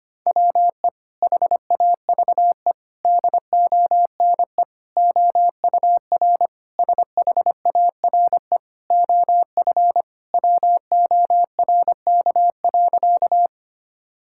Morse code, heard as a keyed tone, about 25 wpm, 700 Hz. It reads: WE HAVE DONE OUR SHARE OF WORK.
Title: Sentences from Top 1000 Words - Review Part 6 - 25wpm